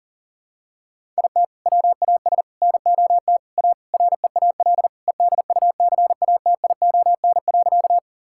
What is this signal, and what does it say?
Morse code, heard as a keyed tone, about 37 wpm, 700 Hz. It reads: IT WAS NOT A REAL EDUCATION.
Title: Sentences from Top 1000 Words - Review Part 6 - 40wpm